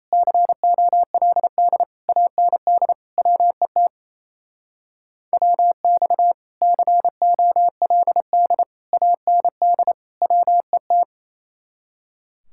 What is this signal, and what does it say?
Morse code, heard as a keyed tone, about 33 wpm, 700 Hz. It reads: COLD AND WET WX COLD AND WET
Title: Speed-Racing Common QSO Phrases Encore - 28wpm